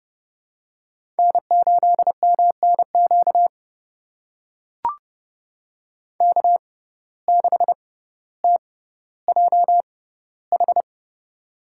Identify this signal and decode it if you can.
Morse code, heard as a keyed tone, about 32 wpm, spaced as for 16 wpm, 700 Hz. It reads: N8MNQ K6TJH
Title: USA Call Signs - 6x Character Spacing - 30wpm